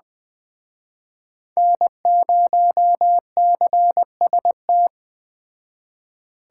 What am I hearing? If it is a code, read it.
Morse code, heard as a keyed tone, about 20 wpm, 700 Hz. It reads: N0CST
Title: Speed Racing - USA 1x3 Call Signs 17wpm